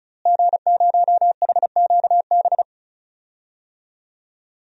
Morse code, heard as a keyed tone, about 36 wpm, 700 Hz. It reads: G0HQB
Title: DXCC-100 Call Signs - 5 Characters - Repeated 3x - 35wpm